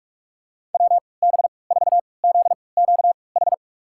Morse code, heard as a keyed tone, about 46 wpm, spaced as for 26 wpm, 700 Hz. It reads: WB4ZQH
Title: USA Call Signs - 6 Characters - 3x Character Spacing - 45wpm